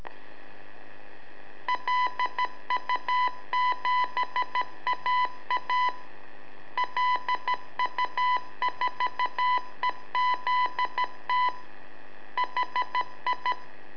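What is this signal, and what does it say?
Morse code, heard as a keyed tone, about 13 wpm, 990 Hz. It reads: LU7AA LU4EZT HI
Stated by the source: Telemetry in Morse from LU7AA and LU4EZ, AMSAT and Turma Mar del Plata, sent from a balloon.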